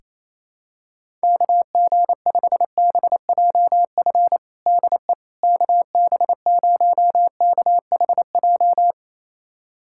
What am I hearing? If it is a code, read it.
Morse code, heard as a keyed tone, about 28 wpm, 700 Hz. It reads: KG5BJF DE KB0KHJ
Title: USA Callsign DE Callsign - 6 Characters - 28wpm